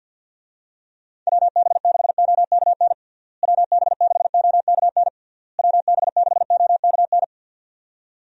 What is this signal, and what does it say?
Morse code, heard as a keyed tone, about 47 wpm, 700 Hz. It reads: WB6OKN WB6OKN WB6OKN
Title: USA Call Signs - 6 Characters - Repeated 3x - 50wpm